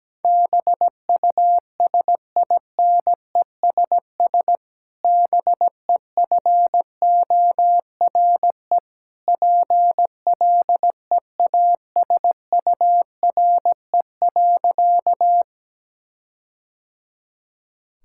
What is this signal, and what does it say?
Morse code, heard as a keyed tone, about 17 wpm, 700 Hz. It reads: BUSINESS BEFORE PLEASURE.
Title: Sentences from Top 1600 Words 17wpm